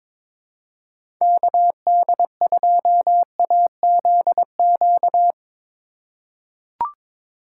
Morse code, heard as a keyed tone, about 22 wpm, 700 Hz. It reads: KD2AZQ
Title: USA Call Signs - 6 Characters - 3x Character Spacing - 22wpm